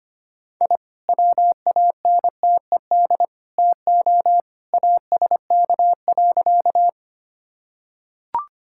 Morse code, heard as a keyed tone, about 25 wpm, 700 Hz. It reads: I WANTED TO ASK.
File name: practice / 25wpm / Sentences with 3-Word Phrases - 25wpm